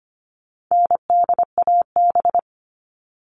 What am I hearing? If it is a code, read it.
Morse code, heard as a keyed tone, about 25 wpm, 700 Hz. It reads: NDAB
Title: All Cut Numbers - 4-Digit 25wpm